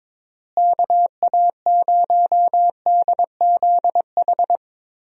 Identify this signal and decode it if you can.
Morse code, heard as a keyed tone, about 22 wpm, 700 Hz. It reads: KA0DZH
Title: USA Call Signs - 6 Characters - Repeated 3x - 22wpm